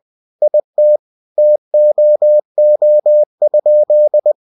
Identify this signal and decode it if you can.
Morse code, heard as a keyed tone, about 20 wpm, 600 Hz. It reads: IT TOO?
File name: cw-books / winnie-the-pooh / winnie-20wpm-page-0051